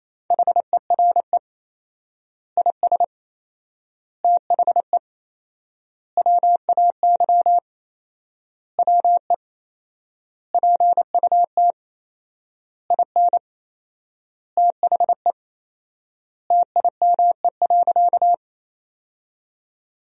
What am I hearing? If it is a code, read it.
Morse code, heard as a keyed tone, about 28 wpm, 700 Hz. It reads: HERE IS THE WAY WE PUT IN THE TIME.
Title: 4x Word-Spacing - Sentences from Top 200 Words - 28wpm